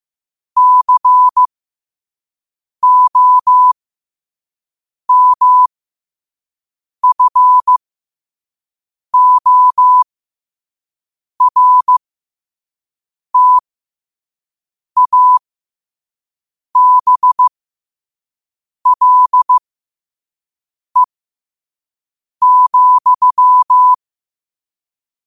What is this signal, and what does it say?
Morse code, heard as a keyed tone, about 15 wpm, spaced as for 5 wpm, 1000 Hz. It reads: COMFORTABLE,
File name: qotdfiles / 5wpm